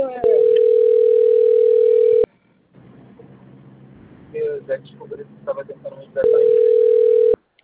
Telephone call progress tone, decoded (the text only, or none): ringback tone